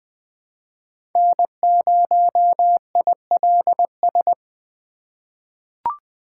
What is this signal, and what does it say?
Morse code, heard as a keyed tone, about 20 wpm, 700 Hz. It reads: N0ILS
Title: USA Call Signs - 5 Characters - 2x Character Spacing - 20wpm